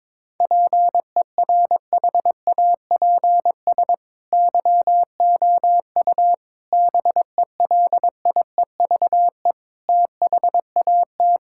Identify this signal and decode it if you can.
Morse code, heard as a keyed tone, about 22 wpm, 700 Hz. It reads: PERHAPS YOU BELIEVE THAT
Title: Sentences from Top 1600 Words 22wpm